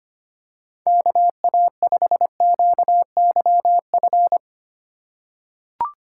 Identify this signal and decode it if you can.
Morse code, heard as a keyed tone, about 25 wpm, 700 Hz. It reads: KA5QYF
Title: USA Call Signs - 6 Characters - 2x Character Spacing - 25wpm